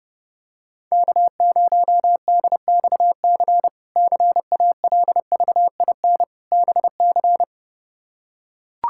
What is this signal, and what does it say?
Morse code, heard as a keyed tone, about 30 wpm, 700 Hz. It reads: K0DXC CALVIN BC
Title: K1USN SST QSO Practice - 30wpm